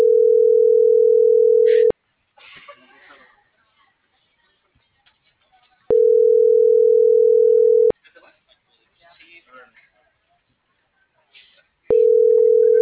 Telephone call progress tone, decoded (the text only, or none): ringback tone